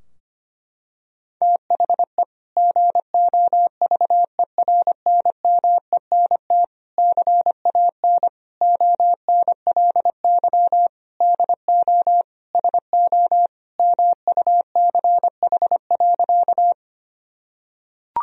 Morse code, heard as a keyed tone, about 25 wpm, 700 Hz. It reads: THE GOVERNMENT CAN ONLY DO SO MUCH.
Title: Sentences from Top 1000 Words - Review Part 3 - 25wpm